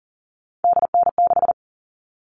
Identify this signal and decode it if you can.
Morse code, heard as a keyed tone, about 40 wpm, 700 Hz. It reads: DN6
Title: All Cut Numbers - 3-Digit 40wpm